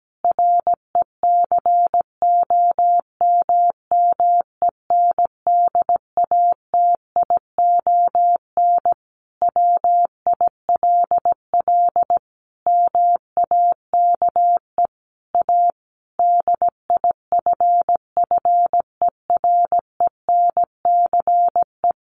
Morse code, heard as a keyed tone, about 17 wpm, 700 Hz. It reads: RECOMMENDATION WILL MAKE A DIFFERENCE
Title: Sentences from Top 1400 Words 17wpm